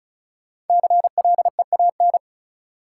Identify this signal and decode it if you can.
Morse code, heard as a keyed tone, about 34 wpm, 700 Hz. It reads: CLEAN
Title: Sentences from Top 1100 Words - New Word Drill - 35wpm